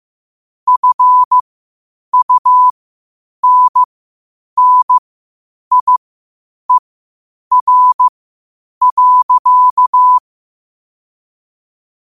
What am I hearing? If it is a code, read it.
Morse code, heard as a keyed tone, about 15 wpm, spaced as for 8 wpm, 1000 Hz. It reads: FUNNIER.
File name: qotdfiles / 8wpm